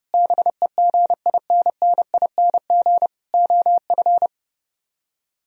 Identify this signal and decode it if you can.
Morse code, heard as a keyed tone, about 29 wpm, 700 Hz. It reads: BEGINNING OF